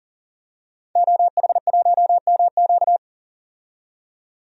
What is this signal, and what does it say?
Morse code, heard as a keyed tone, about 38 wpm, 700 Hz. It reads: OH1MQ
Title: DXCC-100 Call Signs - 5 Characters - 40wpm